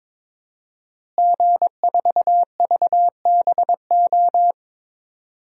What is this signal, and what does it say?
Morse code, heard as a keyed tone, about 22 wpm, 700 Hz. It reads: G4VBO
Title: DXCC-100 Call Signs - 5 Characters - Repeated 3x - 22wpm